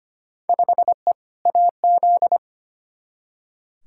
Morse code, heard as a keyed tone, about 25 wpm, 700 Hz. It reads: 5E AZ
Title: Field Day Exchanges - 25wpm